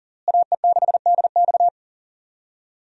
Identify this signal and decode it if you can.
Morse code, heard as a keyed tone, about 40 wpm, 700 Hz. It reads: AE6DX
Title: USA Call Signs - 5 Characters - Repeated 3x - 40wpm